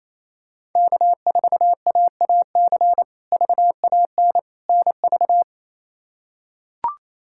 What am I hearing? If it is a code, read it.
Morse code, heard as a keyed tone, about 28 wpm, 700 Hz. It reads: K4AAC VAN NV
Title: K1USN SST QSO Practice - 28wpm